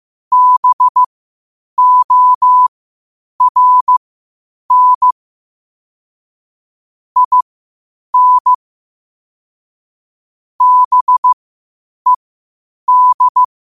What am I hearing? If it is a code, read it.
Morse code, heard as a keyed tone, about 15 wpm, spaced as for 8 wpm, 1000 Hz. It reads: BORN IN BED
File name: qotdfiles / 8wpm